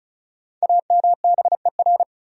Morse code, heard as a keyed tone, about 36 wpm, 700 Hz. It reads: AMBER
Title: USA Common Names - Max 5 Letters - 35wpm